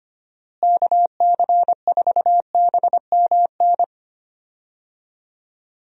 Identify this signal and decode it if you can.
Morse code, heard as a keyed tone, about 25 wpm, 700 Hz. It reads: KC4BMN
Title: USA Call Signs - 6 Characters - Repeated 3x - 25wpm